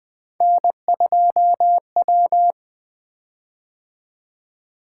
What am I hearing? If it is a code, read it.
Morse code, heard as a keyed tone, about 20 wpm, 700 Hz. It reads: N2W
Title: Three Letter-Number Combinations - 20wpm